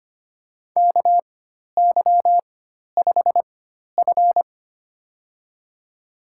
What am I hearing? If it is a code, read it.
Morse code, heard as a keyed tone, about 25 wpm, spaced as for 12 wpm, 700 Hz. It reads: KY5F
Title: USA Call Signs - 4x Character Spacing - 25wpm